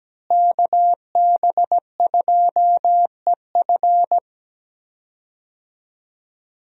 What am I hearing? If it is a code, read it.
Morse code, heard as a keyed tone, about 17 wpm, 700 Hz. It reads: KB2EF
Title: USA Call Signs - 5 Characters - Repeated 2x - 17wpm